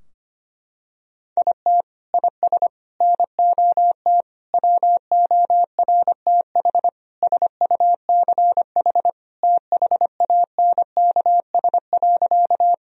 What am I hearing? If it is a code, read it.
Morse code, heard as a keyed tone, about 25 wpm, 700 Hz. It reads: IT IS NOT WORTH SUCH THANKS.
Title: Sentences from Top 1000 Words - Review Part 3 - 25wpm